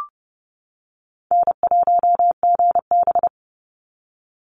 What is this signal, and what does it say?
Morse code, heard as a keyed tone, about 30 wpm, 700 Hz. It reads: N1GB
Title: NA SOTA Chaser Callsigns 30wpm